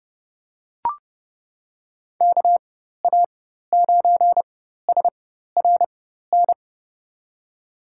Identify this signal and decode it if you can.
Morse code, heard as a keyed tone, about 30 wpm, spaced as for 14 wpm, 700 Hz. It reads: KA9SRN
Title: USA Call Signs - 6 Characters - 4x Character Spacing - 30wpm